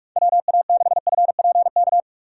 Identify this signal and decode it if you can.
Morse code, heard as a keyed tone, about 45 wpm, 700 Hz. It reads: WA6FPK
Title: USA Call Signs - 6 Characters - 4x Character Spacing - 45wpm